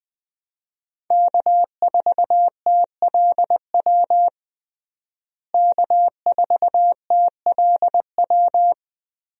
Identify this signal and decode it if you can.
Morse code, heard as a keyed tone, about 20 wpm, 700 Hz. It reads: K4TLW K4TLW
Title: USA Call Signs - 5 Characters - Repeated 2x - 20wpm